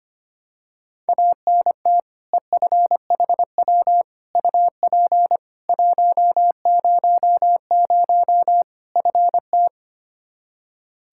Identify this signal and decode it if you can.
Morse code, heard as a keyed tone, about 25 wpm, 700 Hz. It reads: ANT EFHW UP 100 FT
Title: Practice QSO - Exchange 2 - Info - 25wpm